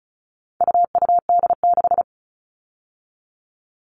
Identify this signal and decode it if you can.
Morse code, heard as a keyed tone, about 34 wpm, 700 Hz. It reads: UUD6
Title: All Cut Numbers - 4-Digit 35wpm